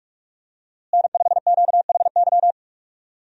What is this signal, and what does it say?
Morse code, heard as a keyed tone, about 41 wpm, 700 Hz. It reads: N5QHY
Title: USA Call Signs - 5 Characters - Repeated 3x - 45wpm